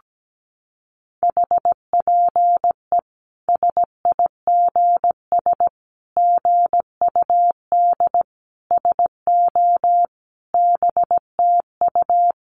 Morse code, heard as a keyed tone, about 17 wpm, 700 Hz. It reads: HPE SIGS GUD SO BTU
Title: Common QSO Phrases 17wpm